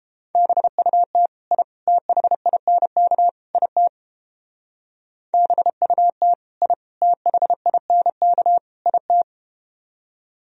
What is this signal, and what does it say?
Morse code, heard as a keyed tone, about 32 wpm, 700 Hz. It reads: BUT I THINK IT BUT I THINK IT